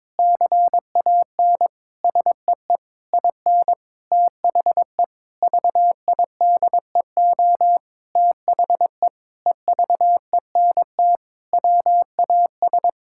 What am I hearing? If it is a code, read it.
Morse code, heard as a keyed tone, about 22 wpm, 700 Hz. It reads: CAN SEE IN THE VIDEO THE EVENT WAS
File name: Sentences with 7-Word Phrases - 22wpm